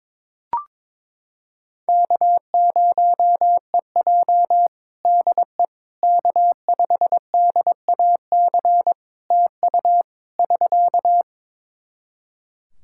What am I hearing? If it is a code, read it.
Morse code, heard as a keyed tone, about 22 wpm, 700 Hz. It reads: K0EJ DE K5DAC TU <SK>
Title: Practice QSO - Exchange 4 - Ending - 22wpm